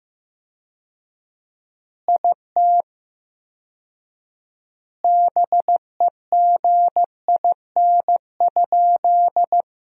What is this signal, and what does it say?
Morse code, heard as a keyed tone, about 15 wpm, 700 Hz. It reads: IT BEGIN?
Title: 4x Word-Spacing - Sentences from Top 300 Words - 15wpm